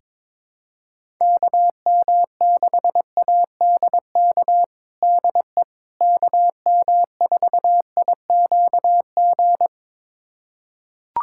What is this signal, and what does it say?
Morse code, heard as a keyed tone, about 22 wpm, 700 Hz. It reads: KM6ADK DE KM4IQG
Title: USA Callsign DE Callsign - 6 Characters - 22wpm